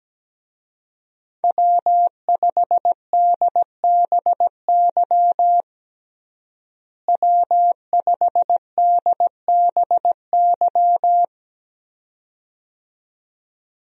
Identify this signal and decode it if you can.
Morse code, heard as a keyed tone, about 17 wpm, 700 Hz. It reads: W5DBY W5DBY
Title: USA Call Signs - 5 Characters - Repeated 2x - 17wpm